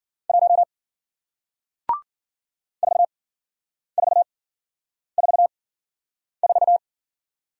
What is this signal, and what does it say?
Morse code, heard as a keyed tone, about 55 wpm, spaced as for 9 wpm, 700 Hz. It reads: 1 4444